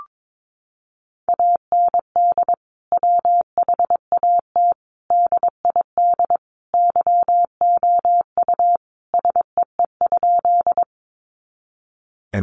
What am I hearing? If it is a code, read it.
Morse code, heard as a keyed tone, about 22 wpm, 700 Hz. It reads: AND WHAT DID YOU SEE?
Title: Sentences from Top 100 Words 22wpm